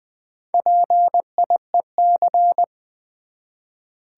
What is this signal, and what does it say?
Morse code, heard as a keyed tone, about 20 wpm, 700 Hz. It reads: PIEC
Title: Build A Word - 1 Letter At A Time - 5-Letter Words - 20wpm